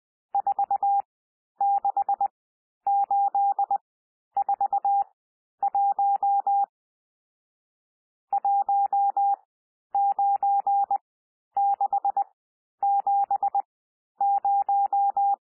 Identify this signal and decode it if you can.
Morse code, heard as a keyed tone, about 20 wpm, spaced as for 11 wpm, 800 Hz. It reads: 46841 19670
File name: cwmp3 / N1 SOUND NUMBERS 20 wpm